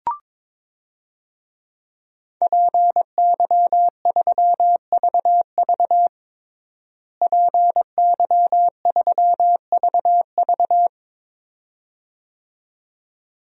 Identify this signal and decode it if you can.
Morse code, heard as a keyed tone, about 22 wpm, 700 Hz. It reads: PY3VV PY3VV
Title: DXCC-100 Call Signs - 5 Characters - Repeated 2x - 22wpm